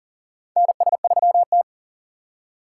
Morse code, heard as a keyed tone, about 39 wpm, 700 Hz. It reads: NS3T
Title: USA Call Signs - 4 Characters - 3x Character Spacing - 40wpm